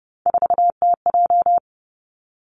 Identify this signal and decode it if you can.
Morse code, heard as a keyed tone, about 31 wpm, 700 Hz. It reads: 4TJ
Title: Three Challenging Letter-Number Combinations 30wpm